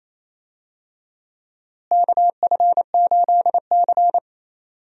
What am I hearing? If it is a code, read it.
Morse code, heard as a keyed tone, about 28 wpm, 700 Hz. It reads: KF8C
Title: USA Call Signs - 4 Characters - Repeated 2x - 28wpm